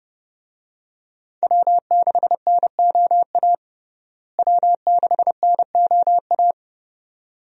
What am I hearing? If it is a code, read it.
Morse code, heard as a keyed tone, about 32 wpm, 700 Hz. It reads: W6NOA W6NOA
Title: USA Call Signs - 5 Characters - Repeated 2x - 30wpm